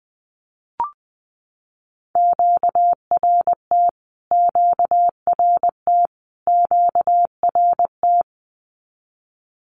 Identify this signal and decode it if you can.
Morse code, heard as a keyed tone, about 20 wpm, 700 Hz. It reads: QRT QRT QRT
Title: Q-Codes Repeated 3x 20wpm